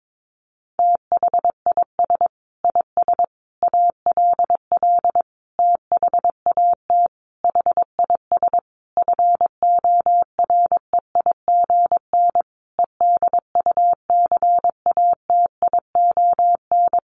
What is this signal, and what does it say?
Morse code, heard as a keyed tone, about 22 wpm, 700 Hz. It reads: THIS IS ALL THAT HIS FOREIGN EDUCATION
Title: Sentences from Top 1400 Words 22wpm